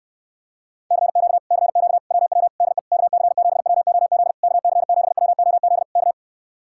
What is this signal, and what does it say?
Morse code, heard as a keyed tone, about 79 wpm, 700 Hz. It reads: CQ CQ WW DE KC7WOZ KC7WOZ K